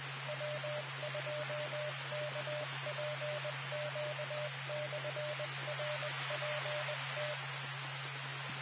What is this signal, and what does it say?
Morse code, heard as a keyed tone, about 20 wpm, 600 Hz. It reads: W2KPQ/RPT
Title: Nassau and Suffolk County Amateur Repeaters